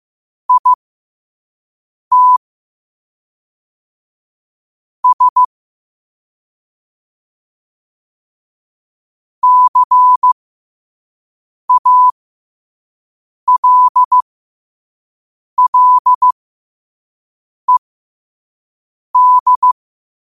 Morse code, heard as a keyed tone, about 15 wpm, spaced as for 5 wpm, 1000 Hz. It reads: IT S CALLED
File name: 5wpm